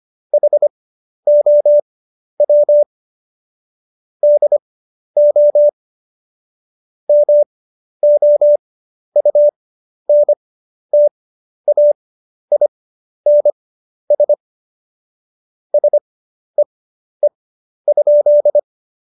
Morse code, heard as a keyed tone, about 25 wpm, spaced as for 11 wpm, 590 Hz. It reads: HOW DO MOUNTAINS SEE?
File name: Jokes-25-char-10-eff